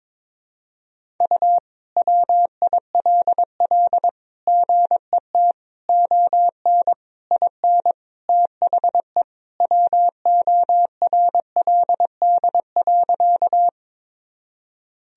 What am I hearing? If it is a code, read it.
Morse code, heard as a keyed tone, about 21 wpm, 700 Hz. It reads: U WILL GET ON IN THE WORLD.
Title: Sentences from Top 200 Words and Abbreviations 22wpm